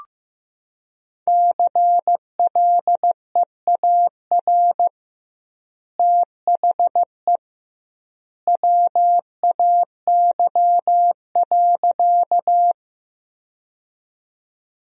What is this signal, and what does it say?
Morse code, heard as a keyed tone, about 15 wpm, 700 Hz. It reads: CLEAR THE WAY.
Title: 2x Word-Spacing - Sentences from Top 500 Words - 15wpm